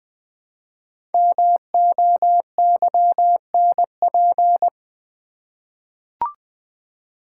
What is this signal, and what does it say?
Morse code, heard as a keyed tone, about 20 wpm, 700 Hz. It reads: MOYNP
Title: Code Groups - 5 Letter:Numbers - 20wpm